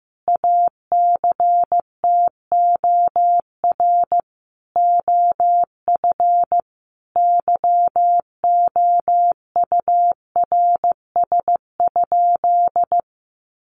Morse code, heard as a keyed tone, about 14 wpm, 700 Hz. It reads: ACTOR OF YOURS?
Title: Sentences from Top 1500 Words 15wpm